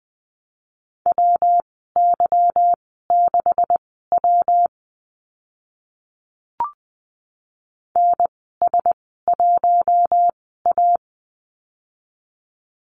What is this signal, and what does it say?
Morse code, heard as a keyed tone, about 20 wpm, spaced as for 15 wpm, 700 Hz. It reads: WY6W NS1A
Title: 167 Lesson 27g - Number 7 - Call Signs 20wpm